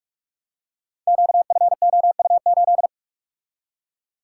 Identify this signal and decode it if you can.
Morse code, heard as a keyed tone, about 44 wpm, 690 Hz. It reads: QFOU8
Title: Code Groups - 5 Letter:Numbers - 45wpm